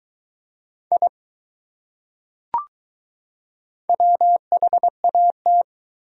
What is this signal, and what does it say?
Morse code, heard as a keyed tone, about 20 wpm, 700 Hz. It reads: I WHAT